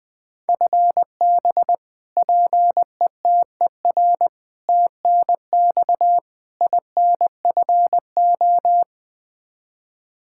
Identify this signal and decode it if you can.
Morse code, heard as a keyed tone, about 20 wpm, 700 Hz. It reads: FB PETER TNX INFO
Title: Practice QSO - Exchange 3 - More Info - 20wpm